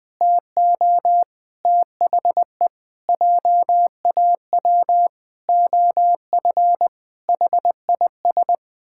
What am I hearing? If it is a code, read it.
Morse code, heard as a keyed tone, about 20 wpm, 700 Hz. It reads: TO THE JAW OF HIS
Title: Pangram Challenge v2 - 20wpm